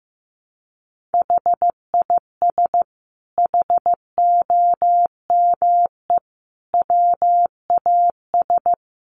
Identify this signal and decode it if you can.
Morse code, heard as a keyed tone, about 15 wpm, 700 Hz. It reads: HIS HOME WAS
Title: Sentences from Top 1500 Words 15wpm